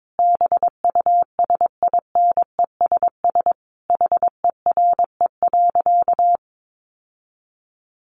Morse code, heard as a keyed tone, about 22 wpm, 700 Hz. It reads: BUSINESS HERE.
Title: Sentences from Top 1300 Words 22wpm